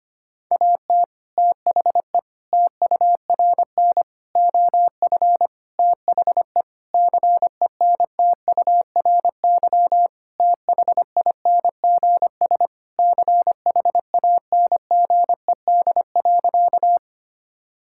Morse code, heard as a keyed tone, about 25 wpm, 700 Hz. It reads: AT THE TURN OF THE CENTURY THINGS CHANGED.
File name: practice / 25wpm / Sentences with 4-Word Phrases - 25wpm